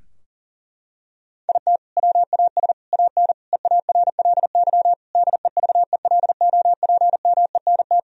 Morse code, heard as a keyed tone, about 37 wpm, 700 Hz. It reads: IT WAS AN EARLY DEVELOPMENT
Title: Sentences from Top 1000 Words - Review Part 4 - 40wpm